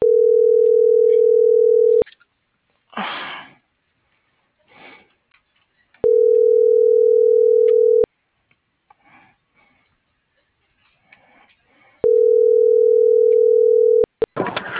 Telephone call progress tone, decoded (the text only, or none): ringback tone